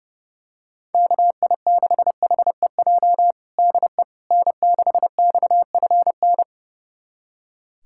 Morse code, heard as a keyed tone, about 30 wpm, 700 Hz. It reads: KI6HEJ DE N6XFN
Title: USA Callsign DE Callsign 30wpm